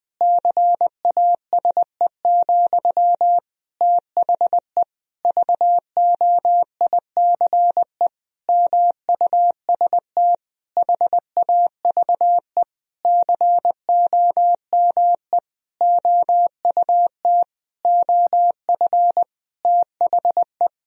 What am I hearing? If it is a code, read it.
Morse code, heard as a keyed tone, about 21 wpm, 700 Hz. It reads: CASE, THE VOICE MUST HAVE COME OUT OF THE
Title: Sentences from Top 1000 Words - Review Part 2 - 20wpm